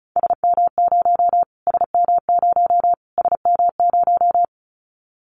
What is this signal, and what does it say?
Morse code, heard as a keyed tone, about 34 wpm, 700 Hz. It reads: SM0 SM0 SM0
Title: Callsign Prefixes for Top 25 DXCC - Repeated 3x - 35wpm